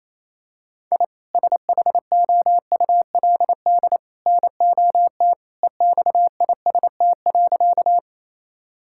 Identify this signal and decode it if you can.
Morse code, heard as a keyed tone, about 28 wpm, 700 Hz. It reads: I SHOULD NOT EXIST.
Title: Sentences from Top 1000 Words - Review Part 5 - 28wpm